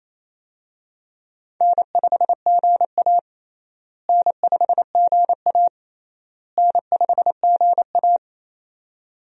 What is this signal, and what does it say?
Morse code, heard as a keyed tone, about 28 wpm, 700 Hz. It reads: N5GA N5GA N5GA
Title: USA Call Signs - 4 Characters - Repeated 3x - 28wpm